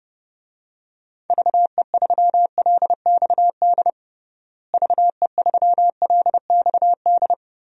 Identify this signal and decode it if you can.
Morse code, heard as a keyed tone, about 30 wpm, 700 Hz. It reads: VE3LXD VE3LXD
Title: DXCC-100 Call Signs - 6 Characters - Repeated 3x - 30wpm